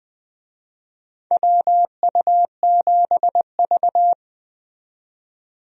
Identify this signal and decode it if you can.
Morse code, heard as a keyed tone, about 20 wpm, 700 Hz. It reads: WU7V
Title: USA Call Signs - 4 Characters - 20wpm